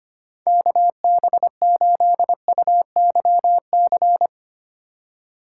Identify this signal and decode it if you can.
Morse code, heard as a keyed tone, about 25 wpm, 700 Hz. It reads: KB8UYC
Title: USA Call Signs - 6 Characters - 3x Character Spacing - 25wpm